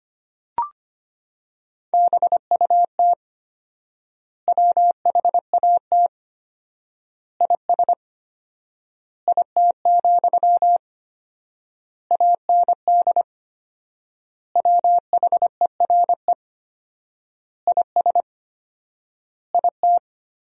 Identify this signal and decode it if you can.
Morse code, heard as a keyed tone, about 25 wpm, 700 Hz. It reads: BUT WHAT IS IT, AND WHERE IS IT
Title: 4x Word-Spacing - Sentences from Top 300 Words - 25wpm